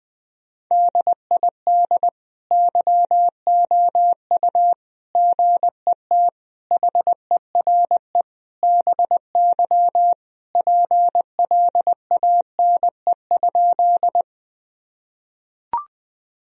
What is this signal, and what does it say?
Morse code, heard as a keyed tone, about 21 wpm, 700 Hz. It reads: DID YOU GET HERE BY PLANE?
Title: Sentences from Top 1000 Words - Review Part 3 - 20wpm